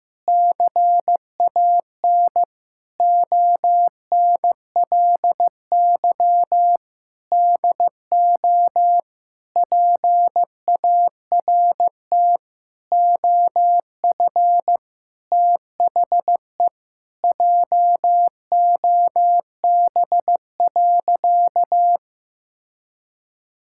Morse code, heard as a keyed tone, about 15 wpm, 700 Hz. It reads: CAN ONLY DO PART OF THE JOB.
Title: Sentences from Top 1000 Words - Review Part 5 - 15wpm